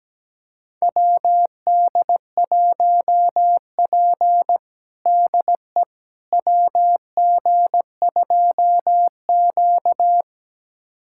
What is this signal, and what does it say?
Morse code, heard as a keyed tone, about 17 wpm, 700 Hz. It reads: WD1P DE WG2Q
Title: USA Callsign DE Callsign - 4 Characters - 17wpm